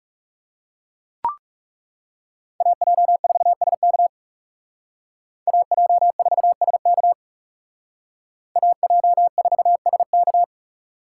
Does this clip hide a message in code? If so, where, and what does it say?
Yes: Morse code from 2.6 to 10.4 s, heard as a keyed tone, about 38 wpm, 700 Hz. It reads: AJ4SK AJ4SK AJ4SK